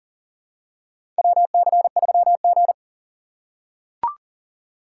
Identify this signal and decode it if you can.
Morse code, heard as a keyed tone, about 37 wpm, 700 Hz. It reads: WC3G
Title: USA Call Signs - 4 Characters - 4x Character Spacing - 40wpm